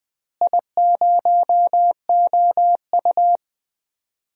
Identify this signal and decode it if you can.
Morse code, heard as a keyed tone, about 20 wpm, 700 Hz. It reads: I0OU
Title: DXCC-100 Call Signs - 4 Characters - Repeated 2x - 20wpm